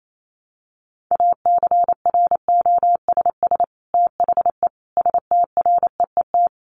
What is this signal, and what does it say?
Morse code, heard as a keyed tone, about 28 wpm, 700 Hz. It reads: ACROSS THE STREET
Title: Sets of 5 Words Encore 28wpm